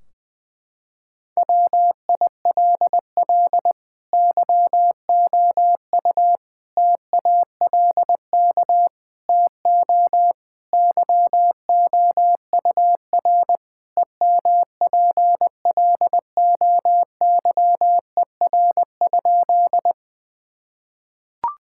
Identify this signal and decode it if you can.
Morse code, heard as a keyed tone, about 20 wpm, 700 Hz. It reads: WILL YOU TALK TO YOUR EMPLOYER?
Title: Sentences from Top 1600 Words 20wpm